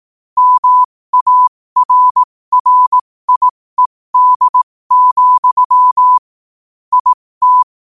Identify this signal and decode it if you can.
Morse code, heard as a keyed tone, about 18 wpm, spaced as for 15 wpm, 1000 Hz. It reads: MARRIED, IT
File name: qotdfiles / 15wpm